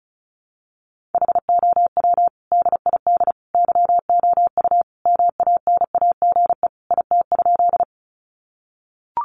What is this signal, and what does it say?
Morse code, heard as a keyed tone, about 34 wpm, 700 Hz. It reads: HOW DID YOU MANAGE IT?
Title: Sentences from Top 1400 Words 35wpm